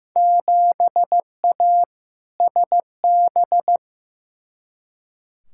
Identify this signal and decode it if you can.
Morse code, heard as a keyed tone, about 15 wpm, 700 Hz. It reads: 7A SB
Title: Field Day Exchanges Encore - 15wpm